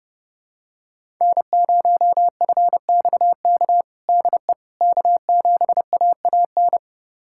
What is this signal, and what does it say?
Morse code, heard as a keyed tone, about 30 wpm, 700 Hz. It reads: N0FXK DE K7AAN
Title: USA Callsign DE Callsign - 5 Characters - 30wpm